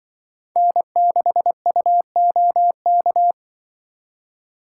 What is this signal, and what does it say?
Morse code, heard as a keyed tone, about 24 wpm, 700 Hz. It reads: N6UOK
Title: Speed Racing - USA 1x3 Call Signs 20wpm